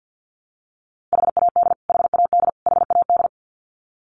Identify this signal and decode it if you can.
Morse code, heard as a keyed tone, about 52 wpm, 700 Hz. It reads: HAD HAD HAD
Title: Top 200 Words Repeated 50wpm